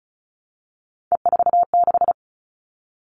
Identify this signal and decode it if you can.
Morse code, heard as a keyed tone, about 34 wpm, 700 Hz. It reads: E46
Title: All Cut Numbers - 3-Digit 35wpm